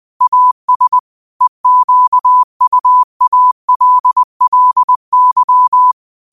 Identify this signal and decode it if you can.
Morse code, heard as a keyed tone, about 20 wpm, 1000 Hz. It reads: AS EQUALLY